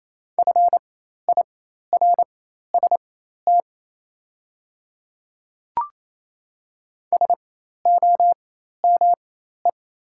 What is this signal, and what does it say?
Morse code, heard as a keyed tone, about 28 wpm, spaced as for 13 wpm, 700 Hz. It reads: FIRST SOME
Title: 4x Character Spacing - Top 100 Words - 28wpm